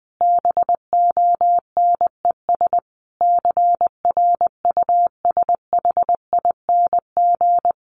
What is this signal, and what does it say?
Morse code, heard as a keyed tone, about 20 wpm, 700 Hz. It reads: BONES CRUSHING
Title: A Sound Of Thunder 20wpm